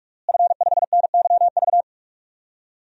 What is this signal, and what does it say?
Morse code, heard as a keyed tone, about 46 wpm, 700 Hz. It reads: F5NYV
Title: DXCC-100 Call Signs - 5 Characters - Repeated 3x - 45wpm